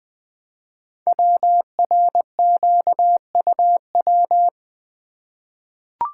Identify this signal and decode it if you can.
Morse code, heard as a keyed tone, about 20 wpm, 700 Hz. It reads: WRQUW
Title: Code Groups - 5 Letters - 20wpm